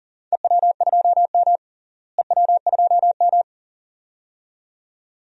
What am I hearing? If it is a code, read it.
Morse code, heard as a keyed tone, about 40 wpm, 700 Hz. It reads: EW2M EW2M
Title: DXCC-100 Call Signs - 4 Characters - Repeated 2x - 40wpm